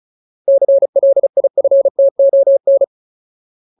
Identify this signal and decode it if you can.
Morse code, heard as a keyed tone, about 34 wpm, 550 Hz. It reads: CLIFTON
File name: Ultimate Word List / 35WPM Ultimate